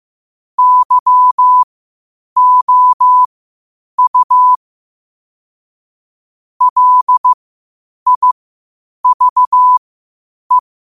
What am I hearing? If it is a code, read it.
Morse code, heard as a keyed tone, about 15 wpm, spaced as for 8 wpm, 1000 Hz. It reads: YOU LIVE
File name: qotdfiles / 8wpm